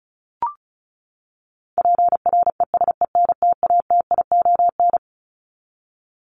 Morse code, heard as a keyed tone, about 34 wpm, 700 Hz. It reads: PRESENTATION
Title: Top 1501-1750 Words 35wpm